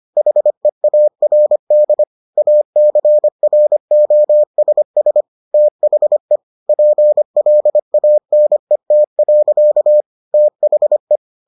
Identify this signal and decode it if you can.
Morse code, heard as a keyed tone, about 25 wpm, 600 Hz. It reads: HEARD ACROSS THE PLANET. THE